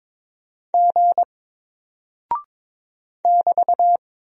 Morse code, heard as a keyed tone, about 22 wpm, 700 Hz. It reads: G <BT>
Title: Single Character (Letter:Number:Punctuation:Prosign) 22wpm